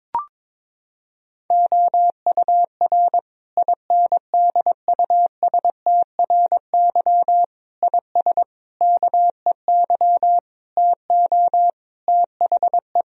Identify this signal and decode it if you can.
Morse code, heard as a keyed tone, about 22 wpm, 700 Hz. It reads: OUR INDUSTRY IS KEY TO THE
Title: Sentences from Top 1000 Words - Review Part 4 - 22wpm